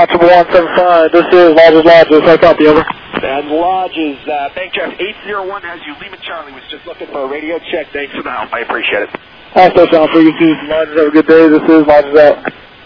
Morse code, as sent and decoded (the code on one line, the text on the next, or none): none